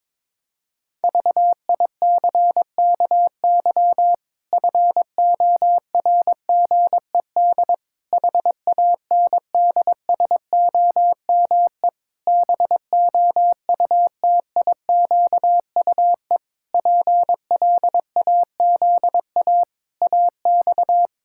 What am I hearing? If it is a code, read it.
Morse code, heard as a keyed tone, about 22 wpm, 700 Hz. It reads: VICKY FORGED HANDSOME BOUTIQUE PLAZA AX
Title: Pangram Challenge v2 - 22wpm